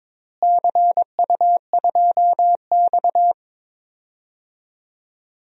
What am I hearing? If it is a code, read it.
Morse code, heard as a keyed tone, about 22 wpm, 700 Hz. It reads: CU2X
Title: DXCC-100 Call Signs - 4 Characters - Repeated 3x - 22wpm